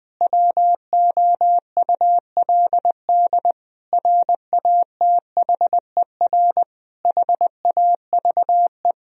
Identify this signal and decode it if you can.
Morse code, heard as a keyed tone, about 20 wpm, 700 Hz. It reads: WOULD RATHER HAVE
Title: Sentences from Top 1000 Words - Review Part 4 - 20wpm